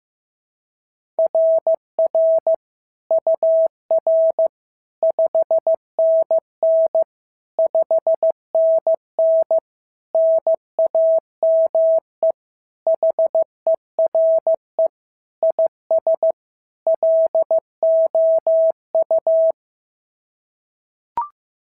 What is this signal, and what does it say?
Morse code, heard as a keyed tone, about 14 wpm, 650 Hz. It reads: RR UR 5NN 5NN NAME HERE IS LOU
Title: SOTA QSO Practice 15wpm